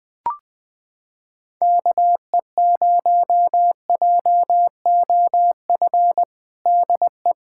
Morse code, heard as a keyed tone, about 21 wpm, 700 Hz. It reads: KE0JOF DE
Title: USA Callsign DE Callsign 20wpm